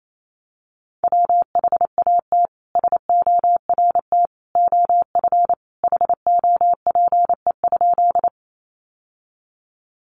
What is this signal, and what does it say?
Morse code, heard as a keyed tone, about 28 wpm, 700 Hz. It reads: WHAT SORT OF HOPE?
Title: Sentences from Top 1300 Words 28wpm